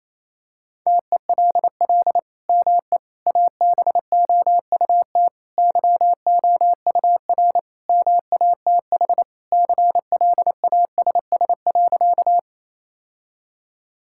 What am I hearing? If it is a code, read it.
Morse code, heard as a keyed tone, about 27 wpm, 700 Hz. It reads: TELL ME ABOUT YOUR MATH CLASS.
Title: Sentences from Top 1000 Words - Review Part 6 - 28wpm